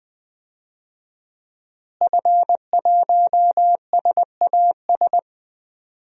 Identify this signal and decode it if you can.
Morse code, heard as a keyed tone, about 20 wpm, 700 Hz. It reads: F1SAS
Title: DXCC-100 Call Signs - 5 Characters - Repeated 3x - 20wpm